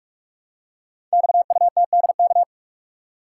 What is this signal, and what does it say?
Morse code, heard as a keyed tone, about 44 wpm, 700 Hz. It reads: XUTDK
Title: Code Groups - 5 Letters - 45wpm